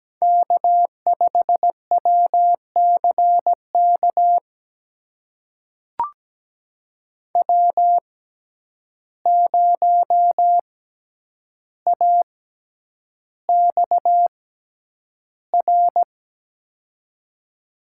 Morse code, heard as a keyed tone, about 17 wpm, spaced as for 9 wpm, 700 Hz. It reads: K5WCK W0AXR
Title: USA Call Signs - 6x Character Spacing - 17wpm